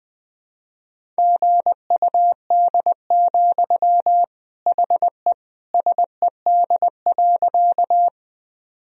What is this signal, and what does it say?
Morse code, heard as a keyed tone, about 20 wpm, 700 Hz. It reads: GUD, HE SED.
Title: Sentences from Top 200 Words and Abbreviations 20wpm